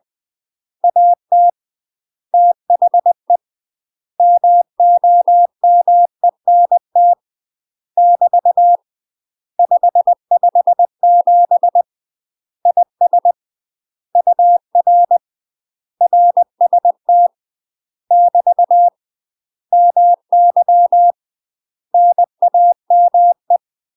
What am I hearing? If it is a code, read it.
Morse code, heard as a keyed tone, about 20 wpm, 700 Hz. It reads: AT THE MOMENT <BT> 557 IS UR RST <BT> MY NAME
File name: UK MASTER QSOs TRAINING / 20WPM / UK MASTER QSOs 0050